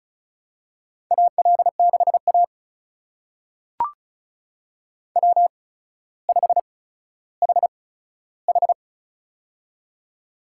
Morse code, heard as a keyed tone, about 36 wpm, spaced as for 15 wpm, 700 Hz. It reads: AL6A W5HH
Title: USA Call Signs - 4 Characters - 8x Character Spacing - 35wpm